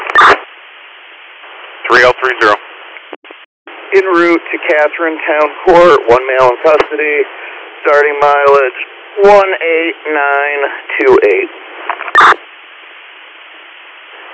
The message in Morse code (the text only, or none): none